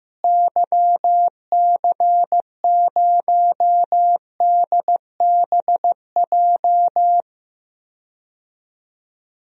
Morse code, heard as a keyed tone, about 15 wpm, 700 Hz. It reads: YC0DBJ
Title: DXCC-100 Call Signs - 6 Characters - Repeated 2x - 15wpm